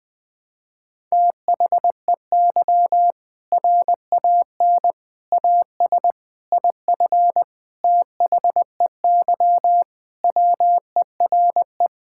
Morse code, heard as a keyed tone, about 21 wpm, 700 Hz. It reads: THEY RAN AS IF THEY WERE
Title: Sentences with 4-Word Phrases - 20wpm